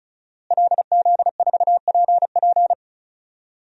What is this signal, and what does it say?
Morse code, heard as a keyed tone, about 36 wpm, 700 Hz. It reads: LZ4PP
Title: DXCC-100 Call Signs - 5 Characters - Repeated 3x - 35wpm